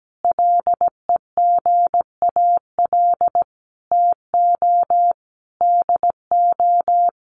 Morse code, heard as a keyed tone, about 17 wpm, 700 Hz. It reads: LEGAL TO DO
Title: Sentences from Top 1200 Words 17wpm